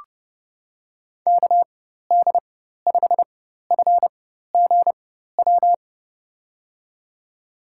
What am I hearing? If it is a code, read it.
Morse code, heard as a keyed tone, about 30 wpm, spaced as for 14 wpm, 700 Hz. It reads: KD5FGW
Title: USA Call Signs - 6 Characters - 4x Character Spacing - 30wpm